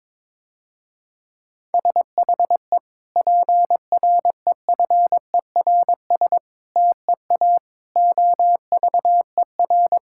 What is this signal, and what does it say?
Morse code, heard as a keyed tone, about 22 wpm, 700 Hz. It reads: SHE PREFERS TEA OVER
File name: Difficult Sentences with Hints - 22wpm